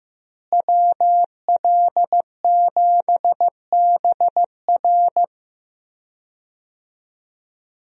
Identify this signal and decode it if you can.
Morse code, heard as a keyed tone, about 15 wpm, 700 Hz. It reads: WL7BR
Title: USA Call Signs - 5 Characters - Repeated 3x - 15wpm